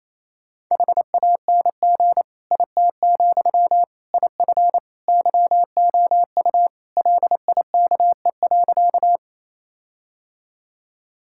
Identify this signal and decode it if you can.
Morse code, heard as a keyed tone, about 28 wpm, 700 Hz. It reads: HANG IT, IF YOU LIKE.
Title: Sentences from Top 1600 Words 28wpm